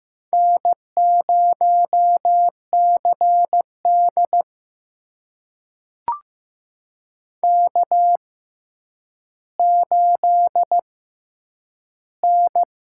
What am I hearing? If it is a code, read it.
Morse code, heard as a keyed tone, about 15 wpm, spaced as for 9 wpm, 700 Hz. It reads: N0CD K8N
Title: USA Call Signs - 4 Characters - 6x Character Spacing - 15wpm